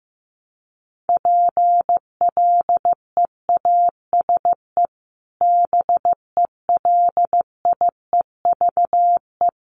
Morse code, heard as a keyed tone, about 15 wpm, 700 Hz. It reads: PLEASE BELIEVE
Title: Sentences from Top 1500 Words 15wpm